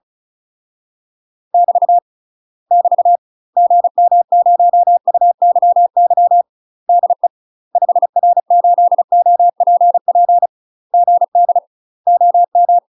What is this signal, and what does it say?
Morse code, heard as a keyed tone, about 36 wpm, 700 Hz. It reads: <BT> <BT> GM0UYY DE 5R8OPP GD OM
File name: UK MASTER QSOs TRAINING / 35WPM / UK MASTER QSOs 0029